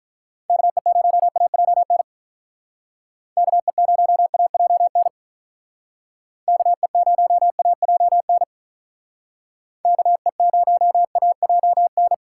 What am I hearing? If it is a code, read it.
Morse code, heard as a keyed tone, about 43 wpm, 700 Hz. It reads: KE0AJN KE0AJN KE0AJN KE0AJN
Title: Speed Racing - USA 2x3 Call Signs 35wpm